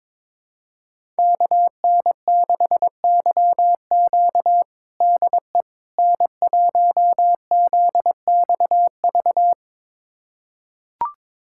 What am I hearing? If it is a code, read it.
Morse code, heard as a keyed tone, about 22 wpm, 700 Hz. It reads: KN6YQ DE N1ZXV
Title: USA Callsign DE Callsign - 5 Characters - 22wpm